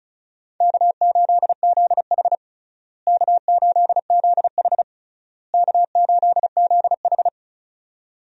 Morse code, heard as a keyed tone, about 36 wpm, 700 Hz. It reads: K8ZH K8ZH K8ZH
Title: USA Call Signs - 4 Characters - Repeated 3x - 35wpm